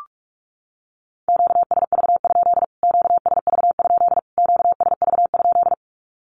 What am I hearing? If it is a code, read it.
Morse code, heard as a keyed tone, about 44 wpm, 700 Hz. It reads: QSV? QSV? QSV?
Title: Q-Codes Repeated 3x 45wpm